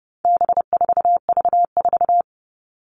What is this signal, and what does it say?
Morse code, heard as a keyed tone, about 30 wpm, 700 Hz. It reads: B4V4
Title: All Cut Numbers - 4-Digit 30wpm